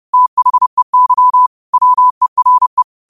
Morse code, heard as a keyed tone, about 30 wpm, 1000 Hz. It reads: THEY WERE
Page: New quotes every day in morse code at 30 Words per minute.